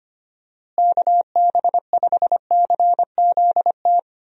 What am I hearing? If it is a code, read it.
Morse code, heard as a keyed tone, about 25 wpm, 700 Hz. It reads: KB5CZT
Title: USA Call Signs - 6 Characters - Repeated 3x - 25wpm